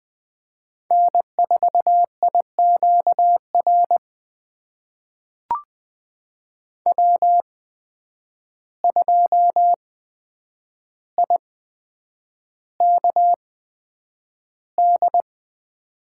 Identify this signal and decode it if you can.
Morse code, heard as a keyed tone, about 20 wpm, spaced as for 9 wpm, 700 Hz. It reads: N4IQR W2IKD
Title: USA Call Signs - 5 Characters - 8x Character Spacing - 20wpm